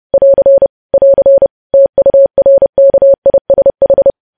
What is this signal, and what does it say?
Morse code, heard as a keyed tone, about 30 wpm, 550 Hz. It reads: <AR> <AR> TURKISH
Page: news headlines in morse code at 30 wpm